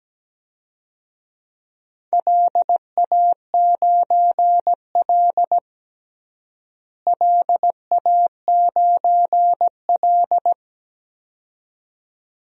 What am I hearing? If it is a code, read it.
Morse code, heard as a keyed tone, about 17 wpm, 700 Hz. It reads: LA9L LA9L
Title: DXCC-100 Call Signs - 4 Characters - Repeated 2x - 17wpm